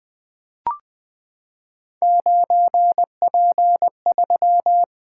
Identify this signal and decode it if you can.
Morse code, heard as a keyed tone, about 20 wpm, 700 Hz. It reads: 9P3
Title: Three Letter-Number Combinations - 20wpm